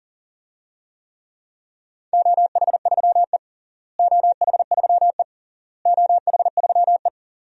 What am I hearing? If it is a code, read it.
Morse code, heard as a keyed tone, about 37 wpm, 700 Hz. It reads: OH3E OH3E OH3E
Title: DXCC-100 Call Signs - 4 Characters - Repeated 3x - 40wpm